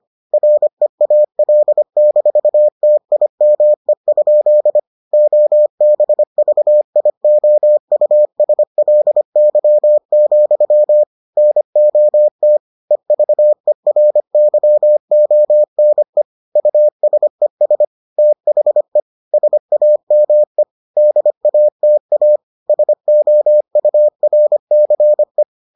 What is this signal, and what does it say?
Morse code, heard as a keyed tone, about 25 wpm, 600 Hz. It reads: REAL-TIME? OBVIOUSLY, NOT EVERYONE USES THE SAME DATA SOURCE